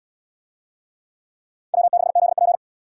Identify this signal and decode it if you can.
Morse code, heard as a keyed tone, about 70 wpm, 700 Hz. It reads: W6ZP